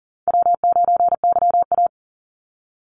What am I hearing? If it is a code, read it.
Morse code, heard as a keyed tone, about 40 wpm, 700 Hz. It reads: W9YA
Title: Easy Call Signs - 40wpm